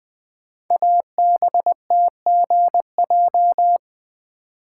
Morse code, heard as a keyed tone, about 20 wpm, 700 Hz. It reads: ABTGJ
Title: Code Groups - 5 Letters - 20wpm